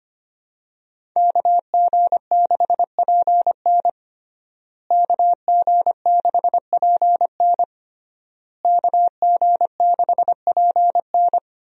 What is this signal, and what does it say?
Morse code, heard as a keyed tone, about 25 wpm, 700 Hz. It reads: KG6PN KG6PN KG6PN
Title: USA Call Signs - 5 Characters - Repeated 3x - 25wpm